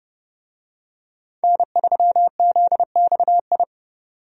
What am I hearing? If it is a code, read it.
Morse code, heard as a keyed tone, about 32 wpm, 700 Hz. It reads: N3ZXI
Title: USA Call Signs - 5 Characters - 30wpm